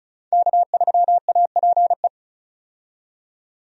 Morse code, heard as a keyed tone, about 36 wpm, 700 Hz. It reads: K3APE
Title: USA Call Signs - 5 Characters - 35wpm